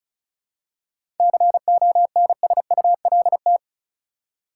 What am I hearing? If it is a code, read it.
Morse code, heard as a keyed tone, about 36 wpm, 700 Hz. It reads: CONSULT
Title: Seven Letter Words - 35wpm